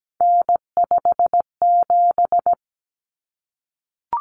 Morse code, heard as a keyed tone, about 17 wpm, 700 Hz. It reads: N57
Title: Common Cut Numbers - 3-Digit 17wpm